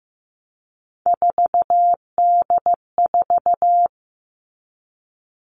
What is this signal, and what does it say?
Morse code, heard as a keyed tone, about 15 wpm, 700 Hz. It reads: 4D4
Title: All Cut Numbers - 3-Digit 15wpm